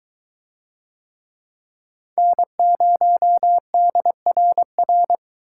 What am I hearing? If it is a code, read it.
Morse code, heard as a keyed tone, about 22 wpm, 700 Hz. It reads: N0DRR